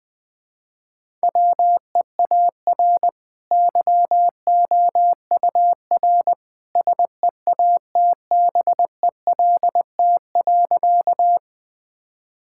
Morse code, heard as a keyed tone, about 20 wpm, 700 Hz. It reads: WEAR YOUR SEATBELT.
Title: Easy Sentences with Hints - 20wpm